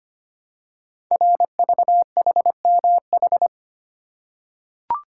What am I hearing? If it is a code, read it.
Morse code, heard as a keyed tone, about 25 wpm, 700 Hz. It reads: RVHMH
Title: Code Groups - 5 Letters - 25wpm